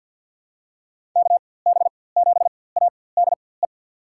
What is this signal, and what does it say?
Morse code, heard as a keyed tone, about 46 wpm, spaced as for 23 wpm, 700 Hz. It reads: KB7ADE
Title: USA Call Signs - 6 Characters - 4x Character Spacing - 50wpm